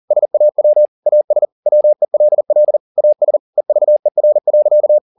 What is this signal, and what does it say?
Morse code, heard as a keyed tone, about 40 wpm, 600 Hz. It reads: SAW AS WELL AS EVER.
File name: the old man and the sea 2 40wpm